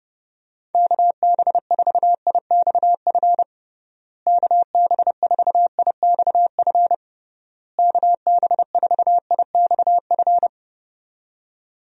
Morse code, heard as a keyed tone, about 30 wpm, 700 Hz. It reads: KB4IXF KB4IXF KB4IXF
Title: USA Call Signs - 6 Characters - Repeated 3x - 30wpm